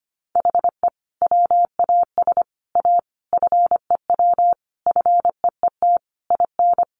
Morse code, heard as a keyed tone, about 25 wpm, 700 Hz. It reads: HE WAS A FEW FEET IN
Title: Sentences from Top 1400 Words 25wpm